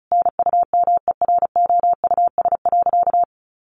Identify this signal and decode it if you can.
Morse code, heard as a keyed tone, about 34 wpm, 700 Hz. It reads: NUMEROUS.
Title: Sentences from Top 1400 Words 35wpm